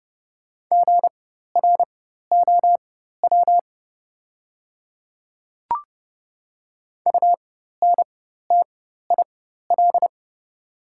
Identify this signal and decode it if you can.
Morse code, heard as a keyed tone, about 30 wpm, spaced as for 14 wpm, 700 Hz. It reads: GROW UNTIL
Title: 4x Character Spacing - Top 300 Words - 30wpm